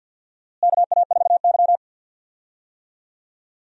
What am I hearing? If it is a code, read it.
Morse code, heard as a keyed tone, about 50 wpm, 700 Hz. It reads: KA4Y
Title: USA Call Signs - 4 Characters - Repeated 2x - 50wpm